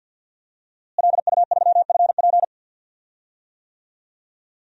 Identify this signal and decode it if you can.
Morse code, heard as a keyed tone, about 47 wpm, 700 Hz. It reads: LU3FP
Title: DXCC-100 Call Signs - 5 Characters - Repeated 2x - 50wpm